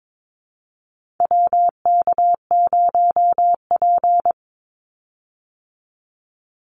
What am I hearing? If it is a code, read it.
Morse code, heard as a keyed tone, about 22 wpm, 700 Hz. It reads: WK0P
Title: Call Signs Repeated 2x - 22wpm